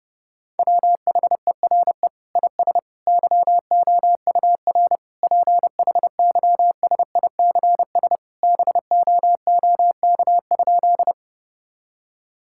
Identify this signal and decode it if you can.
Morse code, heard as a keyed tone, about 32 wpm, 700 Hz. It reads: WHERE IS YOUR PHYSICS BOOK?
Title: Sentences from Top 1000 Words - Review Part 5 - 30wpm